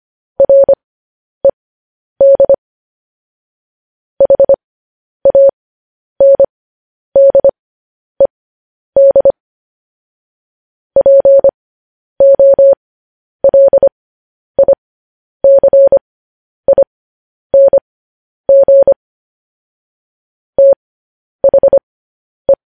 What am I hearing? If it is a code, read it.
Morse code, heard as a keyed tone, about 25 wpm, spaced as for 10 wpm, 550 Hz. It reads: RED HANDED POLICING THE
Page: news-headlines-in-morse-code-at-10-wpm